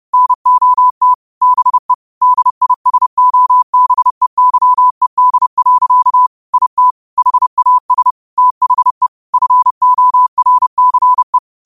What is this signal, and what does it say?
Morse code, heard as a keyed tone, about 30 wpm, 1000 Hz. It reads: NOT BE DISOBEYED. IT HAS THE FORCE